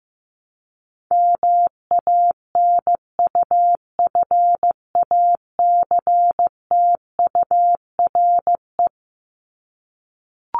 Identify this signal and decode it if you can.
Morse code, heard as a keyed tone, about 15 wpm, 700 Hz. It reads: MANUFACTURE
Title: ING Suffix - Top 100 Words with Base Words 15wpm